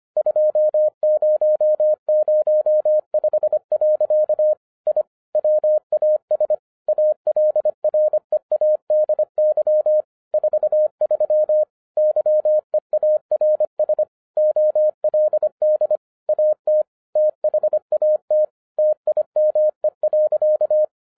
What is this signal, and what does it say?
Morse code, heard as a keyed tone, about 25 wpm, 610 Hz. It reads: 2005. I WAS ALREADY 43 YEARS OLD AT THAT TIME.